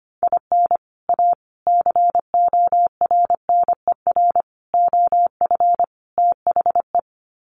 Morse code, heard as a keyed tone, about 25 wpm, 700 Hz. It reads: IN A CORNER OF THE
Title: Sets of 5 Words Encore2 25wpm